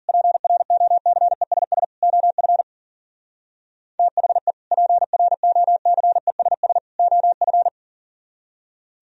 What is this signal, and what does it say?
Morse code, heard as a keyed tone, about 50 wpm, 700 Hz. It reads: PROCESS OF THE PROCESS OF